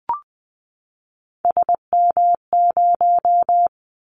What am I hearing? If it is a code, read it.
Morse code, heard as a keyed tone, about 20 wpm, 700 Hz. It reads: SM0
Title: Callsign Prefixes for Top 25 DXCC - 20wpm